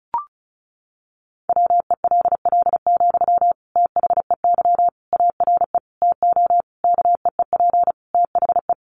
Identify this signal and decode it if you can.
Morse code, heard as a keyed tone, about 34 wpm, 700 Hz. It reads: WELL, THEY ARE TO KEEP THE
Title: Sentences from Top 1500 Words 35wpm